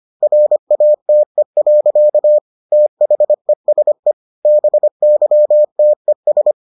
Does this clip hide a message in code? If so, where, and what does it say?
Yes: Morse code from 0.2 to 6.5 s, heard as a keyed tone, about 25 wpm, 600 Hz. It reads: RATE. THESE BYTES